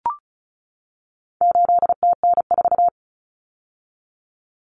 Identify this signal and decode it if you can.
Morse code, heard as a keyed tone, about 34 wpm, 700 Hz. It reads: 8TN4
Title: Common Cut Numbers - 4-Digit 35wpm